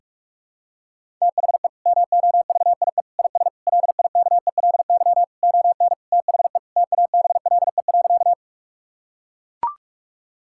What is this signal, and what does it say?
Morse code, heard as a keyed tone, about 41 wpm, 700 Hz. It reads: THE MOVIE IS LIKELY ON THE TABLE.
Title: Sentences from Top 1000 Words - Review Part 5 - 45wpm